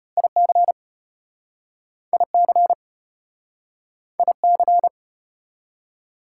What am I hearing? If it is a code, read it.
Morse code, heard as a keyed tone, about 33 wpm, 700 Hz. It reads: IC IC IC